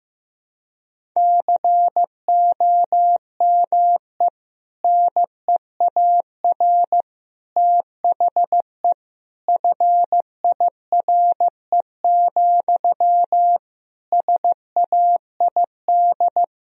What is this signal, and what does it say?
Morse code, heard as a keyed tone, about 15 wpm, 700 Hz. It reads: COME NEAR THE FIRE, SAID
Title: Sentences from Top 1000 Words - Review Part 1 - 15wpm